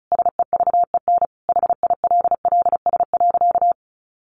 Morse code, heard as a keyed tone, about 34 wpm, 700 Hz. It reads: SEVEN HILLS.
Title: A Sound Of Thunder 35wpm